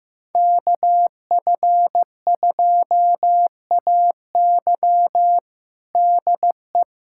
Morse code, heard as a keyed tone, about 15 wpm, 700 Hz. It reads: KF2AY DE
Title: USA Callsign DE Callsign - 5 Characters - 15wpm